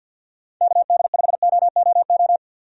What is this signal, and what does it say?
Morse code, heard as a keyed tone, about 46 wpm, 700 Hz. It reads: KD5OOO
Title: USA Call Signs - 6 Characters - 50wpm